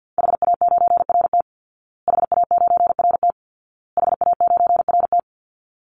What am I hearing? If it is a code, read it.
Morse code, heard as a keyed tone, about 50 wpm, 700 Hz. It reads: HA9RT HA9RT HA9RT
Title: Call Signs Repeated 2x - 50wpm